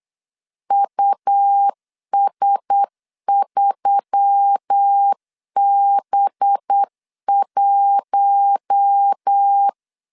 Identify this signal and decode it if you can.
Morse code, heard as a keyed tone, about 8 wpm, 800 Hz. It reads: US3B1
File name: CW / DET / 1 6 1500C S B 8wpm 5142